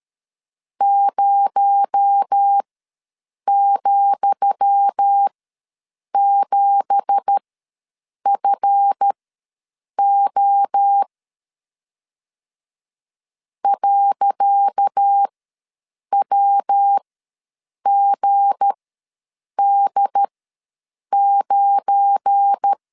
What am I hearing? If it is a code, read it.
Morse code, heard as a keyed tone, about 13 wpm, spaced as for 7 wpm, 800 Hz. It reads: 0,7FO .WGD9
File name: LEZIONI / 10 SIGNS / 10D COMA ws 12wpm 3618